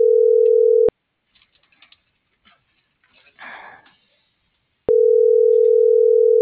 Ringback tone (North America), 440 + 480 Hz.